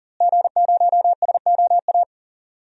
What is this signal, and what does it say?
Morse code, heard as a keyed tone, about 38 wpm, 700 Hz. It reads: G0SOA
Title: DXCC-100 Call Signs - 5 Characters - Repeated 3x - 40wpm